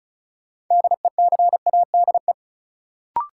Morse code, heard as a keyed tone, about 36 wpm, 700 Hz. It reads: DECADE
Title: Sets of 5 Words - New Word Drill - 35wpm